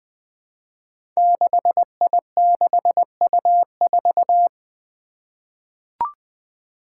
Morse code, heard as a keyed tone, about 20 wpm, 700 Hz. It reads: 6I6U4
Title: Five Challenging Letter-Number Combinations 20wpm